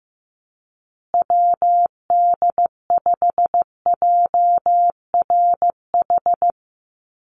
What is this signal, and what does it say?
Morse code, heard as a keyed tone, about 15 wpm, 700 Hz. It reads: WD5JRH
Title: USA Call Signs Repeated 2x 15wpm